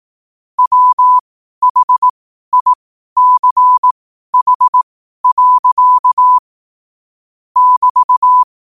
Morse code, heard as a keyed tone, about 18 wpm, spaced as for 12 wpm, 1000 Hz. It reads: WHICH. <BT>
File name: qotdfiles / 12wpm